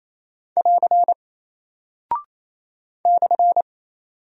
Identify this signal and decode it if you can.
Morse code, heard as a keyed tone, about 28 wpm, 700 Hz. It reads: <AR> /
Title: Single Character (Letter:Number:Punctuation:Prosign) 28wpm